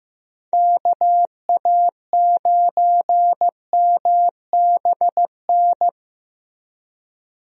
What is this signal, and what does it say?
Morse code, heard as a keyed tone, about 15 wpm, 700 Hz. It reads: KA9MBN
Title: USA Call Signs - 6 Characters - Repeated 2x - 15wpm